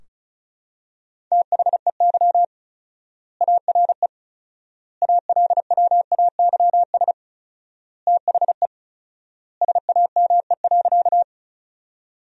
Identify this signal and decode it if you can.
Morse code, heard as a keyed tone, about 34 wpm, 700 Hz. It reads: THEY ARE ALWAYS THE SAME.
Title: 4x Word-Spacing - Sentences from Top 300 Words - 35wpm